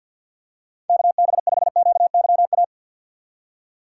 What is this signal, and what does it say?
Morse code, heard as a keyed tone, about 53 wpm, 700 Hz. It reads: KB5QYA
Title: USA Call Signs - 6 Characters - 6x Character Spacing - 50wpm